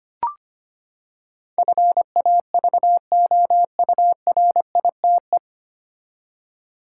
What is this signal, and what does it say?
Morse code, heard as a keyed tone, about 25 wpm, 700 Hz. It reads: FAVOURITE
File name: Nine Letter Words - 25wpm